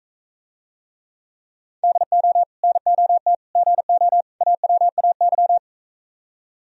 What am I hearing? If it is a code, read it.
Morse code, heard as a keyed tone, about 45 wpm, 700 Hz. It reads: DO NOT GO AWAY